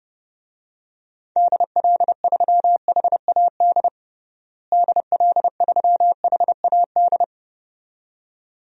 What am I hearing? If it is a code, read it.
Morse code, heard as a keyed tone, about 30 wpm, 700 Hz. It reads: DL3HAD DL3HAD
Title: DXCC-100 Call Signs - 6 Characters - Repeated 2x - 30wpm